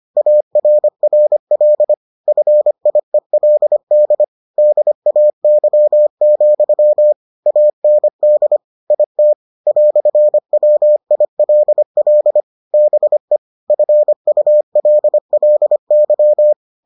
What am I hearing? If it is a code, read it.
Morse code, heard as a keyed tone, about 25 wpm, 600 Hz. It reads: ARRL FIELD DAY, AND IT "WILL BE FULLY